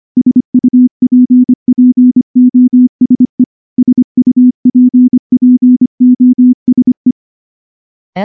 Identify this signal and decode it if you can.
Morse code, heard as a keyed tone, about 24 wpm, 250 Hz. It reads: SUPPOSE SUPPOSE
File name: morse-training-25WPM